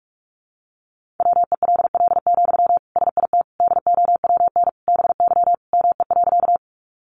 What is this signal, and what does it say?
Morse code, heard as a keyed tone, about 44 wpm, 700 Hz. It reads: WELL, SIT DOWN BY ME.
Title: Sentences from Top 500 Words 45wpm